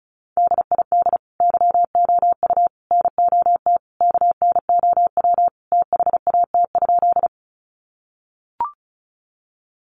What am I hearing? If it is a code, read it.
Morse code, heard as a keyed tone, about 34 wpm, 700 Hz. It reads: DID YOU NOT KNOW THAT?
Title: Sentences from Top 600 Words 35wpm